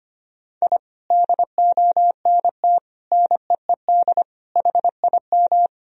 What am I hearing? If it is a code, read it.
Morse code, heard as a keyed tone, about 25 wpm, 700 Hz. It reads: I DONT NEED HIM
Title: Sentences from Top 1000 Words - Review Part 1 - 25wpm